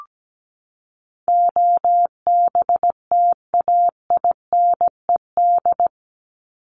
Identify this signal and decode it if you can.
Morse code, heard as a keyed tone, about 17 wpm, 700 Hz. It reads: OBTAINED
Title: ED Suffix - Top 100 Words with Base Words 17wpm